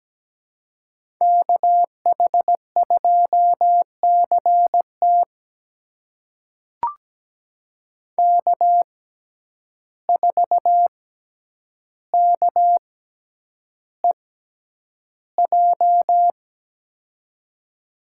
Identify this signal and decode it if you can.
Morse code, heard as a keyed tone, about 17 wpm, spaced as for 9 wpm, 700 Hz. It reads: KH2CT K4KEJ
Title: USA Call Signs - 5 Characters - 6x Character Spacing - 17wpm